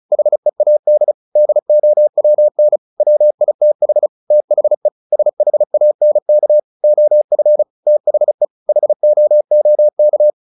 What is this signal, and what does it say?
Morse code, heard as a keyed tone, about 34 wpm, 600 Hz. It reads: HEAD DOWN WITH THE SHANK OF THE HOOK
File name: the old man and the sea / the old man and the sea 1 35wpm